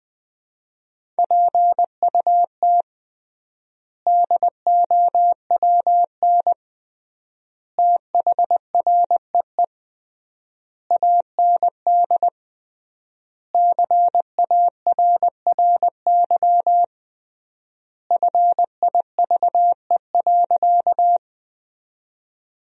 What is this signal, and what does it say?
Morse code, heard as a keyed tone, about 20 wpm, 700 Hz. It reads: PUT DOWN THREE AND CARRY FIVE.
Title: 3x Word-Spacing - Sentences from Top 400 Words - 20wpm